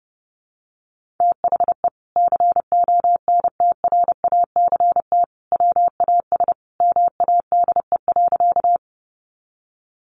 Morse code, heard as a keyed tone, about 30 wpm, 700 Hz. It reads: THE CONTRACT WAS MADE.
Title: Sentences from Top 1400 Words 30wpm